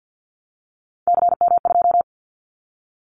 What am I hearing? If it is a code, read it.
Morse code, heard as a keyed tone, about 50 wpm, 700 Hz. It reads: CM2
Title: Callsign Prefixes for Top 100 DXCC - 50wpm